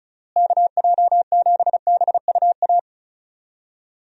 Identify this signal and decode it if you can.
Morse code, heard as a keyed tone, about 34 wpm, 700 Hz. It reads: KJ7BUA
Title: USA Call Signs - 6 Characters - Repeated 3x - 35wpm